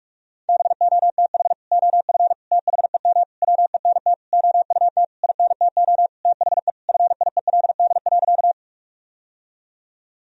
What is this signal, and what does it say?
Morse code, heard as a keyed tone, about 46 wpm, 700 Hz. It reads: BOTH OF THEM WENT OUT INTO THE FIELD.
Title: Sentences from Top 1000 Words - Review Part 2 - 45wpm